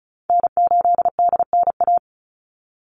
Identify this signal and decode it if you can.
Morse code, heard as a keyed tone, about 34 wpm, 700 Hz. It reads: N8DNA
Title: Call Signs Repeated 3x - 35wpm